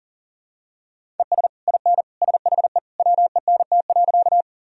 Morse code, heard as a keyed tone, about 37 wpm, 700 Hz. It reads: ES IN SHE WENT.
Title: Sentences from Top 200 Words and Abbreviations 40wpm